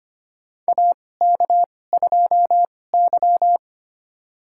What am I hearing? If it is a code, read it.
Morse code, heard as a keyed tone, about 25 wpm, spaced as for 18 wpm, 700 Hz. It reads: AK2Y
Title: USA Call Signs - 4 Characters - 2x Character Spacing - 25wpm